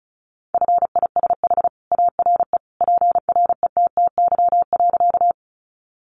Morse code, heard as a keyed tone, about 34 wpm, 700 Hz. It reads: FISH ARE PRETTY.
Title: Additional Easy Phrases 35wpm